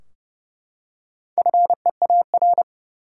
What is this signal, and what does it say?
Morse code, heard as a keyed tone, about 30 wpm, 700 Hz. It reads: FEAR
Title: Four Letter Words - 30wpm